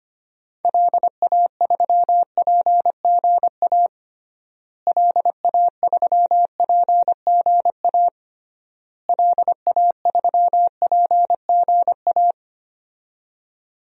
Morse code, heard as a keyed tone, about 25 wpm, 700 Hz. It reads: LA3PGA LA3PGA LA3PGA
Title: DXCC-100 Call Signs - 6 Characters - Repeated 3x - 25wpm